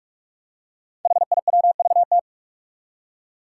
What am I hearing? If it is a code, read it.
Morse code, heard as a keyed tone, about 44 wpm, 700 Hz. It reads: HIWVT
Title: Five Challenging Letter-Number Combinations 45wpm